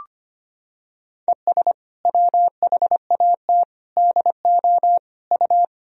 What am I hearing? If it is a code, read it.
Morse code, heard as a keyed tone, about 25 wpm, 700 Hz. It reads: ES WHAT DO U
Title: Sentences from Top 100 Words and Abbreviations 25wpm